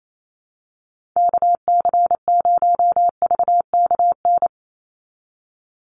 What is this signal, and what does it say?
Morse code, heard as a keyed tone, about 28 wpm, 700 Hz. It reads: KC0VKN
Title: Call Signs Repeated 3x - 28wpm